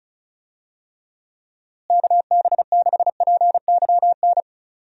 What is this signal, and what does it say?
Morse code, heard as a keyed tone, about 34 wpm, 700 Hz. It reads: KB6PYN
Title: USA Call Signs - 6 Characters - Repeated 2x - 35wpm